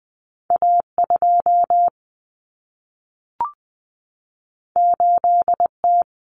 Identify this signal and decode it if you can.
Morse code, heard as a keyed tone, about 20 wpm, 700 Hz. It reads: A2 8T
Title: Common Cut Numbers - 2-Digit 20wpm